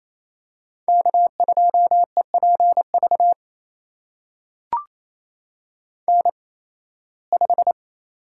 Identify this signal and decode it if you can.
Morse code, heard as a keyed tone, about 28 wpm, spaced as for 18 wpm, 700 Hz. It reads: K2EPV N5
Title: USA Call Signs - 8x Character Spacing - 28wpm